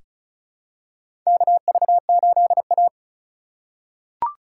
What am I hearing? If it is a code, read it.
Morse code, heard as a keyed tone, about 36 wpm, 700 Hz. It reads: KV8A
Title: USA Call Signs - 4 Characters - 4x Character Spacing - 35wpm